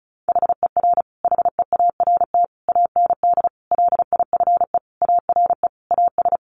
Morse code, heard as a keyed tone, about 34 wpm, 700 Hz. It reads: HER HEART AND LIFE ARE AS
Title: Sentences from Top 1500 Words 35wpm